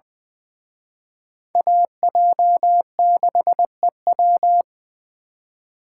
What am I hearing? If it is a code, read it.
Morse code, heard as a keyed tone, about 23 wpm, 700 Hz. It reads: AJ6EW
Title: Speed Racing - USA 2x2 Call Signs 20wpm